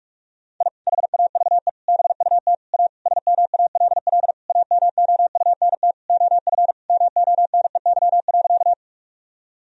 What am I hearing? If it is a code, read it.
Morse code, heard as a keyed tone, about 46 wpm, 700 Hz. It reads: I HAVE BUT A SMALL AMOUNT OF MONEY.
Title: Sentences from Top 1000 Words - Review Part 4 - 45wpm